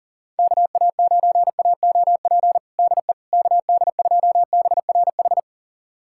Morse code, heard as a keyed tone, about 40 wpm, 700 Hz. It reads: KA9AOP DE KD2BRH
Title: USA Callsign DE Callsign - 6 Characters - 40wpm